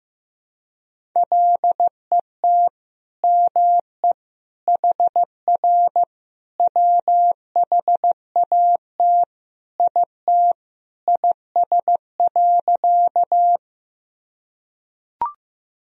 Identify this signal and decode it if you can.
Morse code, heard as a keyed tone, about 15 wpm, 700 Hz. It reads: LET ME HR WHAT IT IS.
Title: Sentences from Top 300 Words and Abbreviations 15wpm